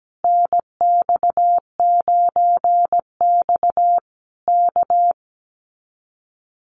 Morse code, heard as a keyed tone, about 17 wpm, 700 Hz. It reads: NX9X K
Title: Common QSO Phrases Encore 17wpm